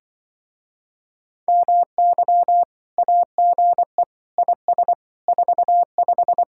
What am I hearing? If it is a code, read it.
Morse code, heard as a keyed tone, about 25 wpm, 700 Hz. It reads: MY AGE IS 45
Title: Speed-Racing Common QSO Phrases - 20wpm